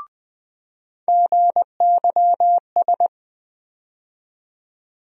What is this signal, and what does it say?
Morse code, heard as a keyed tone, about 21 wpm, 700 Hz. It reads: GYS
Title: USA Call Sign Postfixes - 3 Letters 20wpm